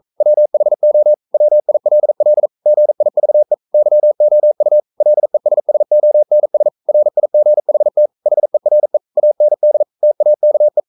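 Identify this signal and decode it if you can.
Morse code, heard as a keyed tone, about 42 wpm, 600 Hz. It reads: WHO WILL GIVE YOU LESSONS RIGHT HERE AND TAKE
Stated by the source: Morse code sending of an English novel at 42 wpm.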